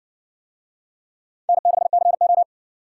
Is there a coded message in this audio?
Yes: Morse code from 1.5 to 2.4 s, heard as a keyed tone, about 58 wpm, 700 Hz. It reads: N6CO